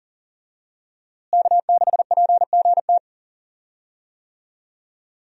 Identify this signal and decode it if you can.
Morse code, heard as a keyed tone, about 40 wpm, 700 Hz. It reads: K6PGT
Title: USA Call Signs - 5 Characters - Repeated 3x - 40wpm